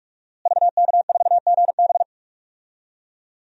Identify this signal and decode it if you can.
Morse code, heard as a keyed tone, about 43 wpm, 700 Hz. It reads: VK4GB
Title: DXCC-100 Call Signs - 5 Characters - Repeated 2x - 45wpm